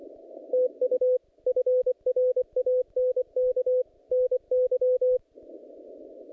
Morse code, heard as a keyed tone, about 24 wpm, 510 Hz. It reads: TU FRANK NY
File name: contests / naqp-cw-k5tr-op-k5pi- -0100